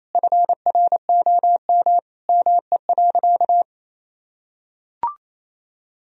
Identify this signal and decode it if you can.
Morse code, heard as a keyed tone, about 28 wpm, 700 Hz. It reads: FROM ME.
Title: Sentences from Top 1000 Words - Review Part 2 - 28wpm